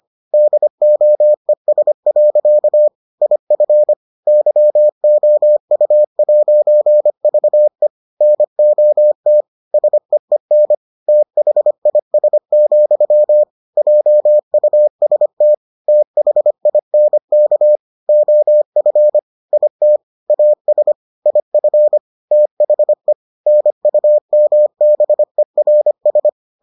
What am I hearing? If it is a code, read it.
Morse code, heard as a keyed tone, about 25 wpm, 600 Hz. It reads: DOES. IF YOU'VE NOT SEEN THIS, JUST THINK OF IT AS IF THE NUMBERS